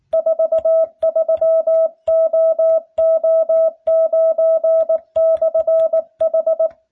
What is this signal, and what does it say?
Morse code, heard as a keyed tone, about 19 wpm, 650 Hz. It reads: 43OO9/H